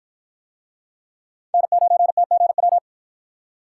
Morse code, heard as a keyed tone, about 53 wpm, 700 Hz. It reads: N9TGW